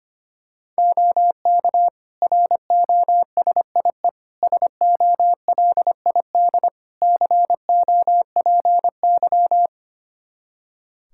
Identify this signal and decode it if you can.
Morse code, heard as a keyed tone, about 25 wpm, 700 Hz. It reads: OK ROSIE SOLID COPY
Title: Practice QSO - Exchange 3 - More Info - 25wpm